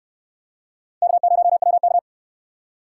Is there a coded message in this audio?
Yes: Morse code from 1.0 to 2.0 s, heard as a keyed tone, about 68 wpm, 700 Hz. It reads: K0WK